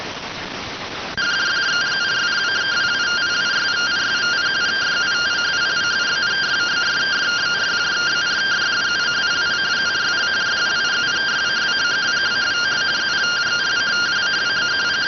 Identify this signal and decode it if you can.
Two-tone teleprinter signal, about 170 Hz shift.